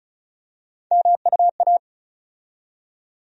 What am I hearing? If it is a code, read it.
Morse code, heard as a keyed tone, about 34 wpm, 700 Hz. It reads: MUA
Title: USA Call Sign Postfixes - 3 Letters 35wpm